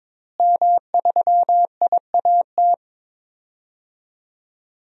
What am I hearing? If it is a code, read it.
Morse code, heard as a keyed tone, about 22 wpm, 700 Hz. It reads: M3IAT
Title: DXCC-100 Call Signs - 5 Characters - Repeated 2x - 22wpm